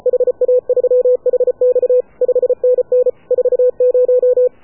Morse code, heard as a keyed tone, about 33 wpm, 500 Hz. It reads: HA3HX 5NN 40
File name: AUDIO2019 / HA7UI CW